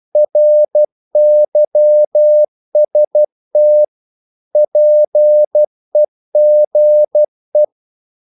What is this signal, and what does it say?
Morse code, heard as a keyed tone, about 12 wpm, 600 Hz. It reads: RYST PEGE
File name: Korte ord DK 12wpm